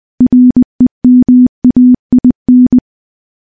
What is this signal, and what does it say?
Morse code, heard as a keyed tone, about 21 wpm, 250 Hz. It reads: REMAIN
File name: morse / morse-training-20WPM